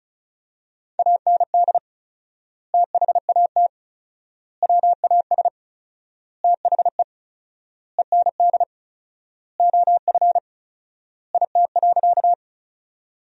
Morse code, heard as a keyed tone, about 34 wpm, 700 Hz. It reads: AND THAT WAS THE END OF IT.
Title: 4x Word-Spacing - Sentences from Top 200 Words - 35wpm